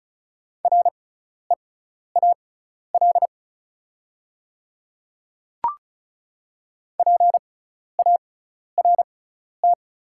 Morse code, heard as a keyed tone, about 36 wpm, spaced as for 12 wpm, 690 Hz. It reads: REAL PART
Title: 6x Character Spacing - Top 300 Words - 35wpm